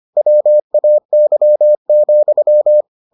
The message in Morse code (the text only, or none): WAY,